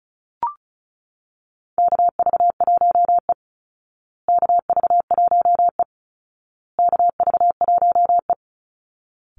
Morse code, heard as a keyed tone, about 34 wpm, 700 Hz. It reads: KV1E KV1E KV1E
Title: Call Signs Repeated 2x - 35wpm